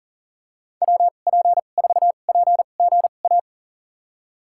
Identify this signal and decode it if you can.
Morse code, heard as a keyed tone, about 40 wpm, spaced as for 29 wpm, 700 Hz. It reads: WP4PGA
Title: USA Call Signs - 6 Characters - 2x Character Spacing - 40wpm